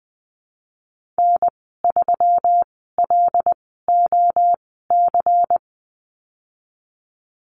Morse code, heard as a keyed tone, about 20 wpm, spaced as for 15 wpm, 700 Hz. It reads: N3LOC
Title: 140 Lesson 23g - Letter f - Call Signs 20wpm